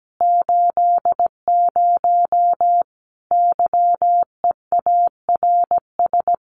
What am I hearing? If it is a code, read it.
Morse code, heard as a keyed tone, about 17 wpm, 700 Hz. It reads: 80 YEARS
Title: Common QSO Phrases 17wpm